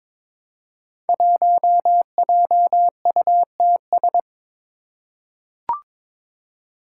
Morse code, heard as a keyed tone, about 22 wpm, 700 Hz. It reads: 1JUTS
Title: Five Challenging Letter-Number Combinations 22wpm